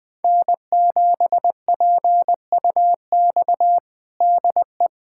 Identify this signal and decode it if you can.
Morse code, heard as a keyed tone, about 20 wpm, 700 Hz. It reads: N7PUX DE
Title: USA Callsign DE Callsign - 5 Characters - 20wpm